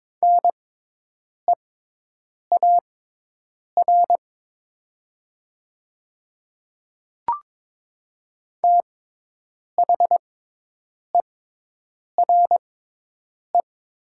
Morse code, heard as a keyed tone, about 22 wpm, spaced as for 8 wpm, 700 Hz. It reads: NEAR THERE
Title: 6x Character Spacing - Top 200 Words - 22wpm